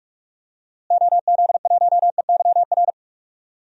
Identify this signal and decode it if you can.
Morse code, heard as a keyed tone, about 46 wpm, 700 Hz. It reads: OZ1EYR
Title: DXCC-100 Call Signs - 6 Characters - Repeated 2x - 45wpm